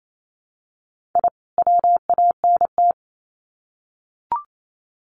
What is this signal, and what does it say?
Morse code, heard as a keyed tone, about 28 wpm, 700 Hz. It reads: I WANT
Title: Sets of 2 Words 28wpm